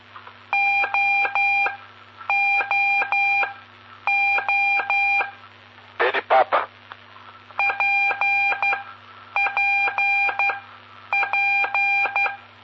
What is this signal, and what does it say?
Morse code, heard as a keyed tone, about 12 wpm, spaced as for 8 wpm, 2400 Hz, with noise. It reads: OOO PPP
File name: pp5bar / aulas de telegrafia / Aula-12